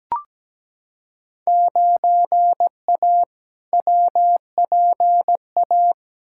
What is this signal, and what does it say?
Morse code, heard as a keyed tone, about 17 wpm, 700 Hz. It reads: 9A WPA
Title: Field Day Exchanges Encore - 17wpm